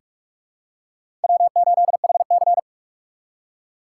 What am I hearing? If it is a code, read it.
Morse code, heard as a keyed tone, about 44 wpm, 700 Hz. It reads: W8HC
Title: USA Call Signs - 4x Character Spacing - 45wpm